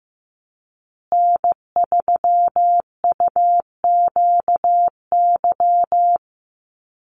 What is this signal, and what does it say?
Morse code, heard as a keyed tone, about 15 wpm, 700 Hz. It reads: N3UQY
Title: USA Call Signs Repeated 3x 15wpm